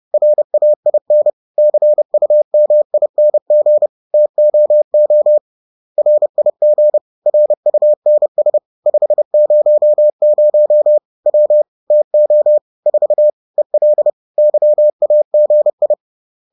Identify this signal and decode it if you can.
Morse code, heard as a keyed tone, about 30 wpm, 600 Hz. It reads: RAIN CUMING TOO RIG RUNS 500 W TO 4 EL YAGI
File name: 30wpm-4